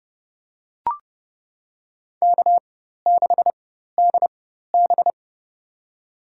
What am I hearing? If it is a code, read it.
Morse code, heard as a keyed tone, about 30 wpm, spaced as for 14 wpm, 700 Hz. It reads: K6DB
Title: USA Call Signs - 4 Characters - 4x Character Spacing - 30wpm